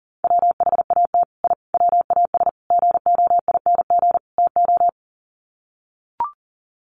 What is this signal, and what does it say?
Morse code, heard as a keyed tone, about 40 wpm, 700 Hz. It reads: WHAT I WAS GOING TO
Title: Sets of 5 Words Encore2 40wpm